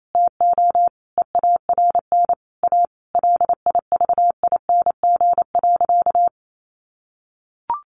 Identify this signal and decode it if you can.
Morse code, heard as a keyed tone, about 28 wpm, 700 Hz. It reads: TO EARN A LIVING.
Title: Sentences from Top 1500 Words 28wpm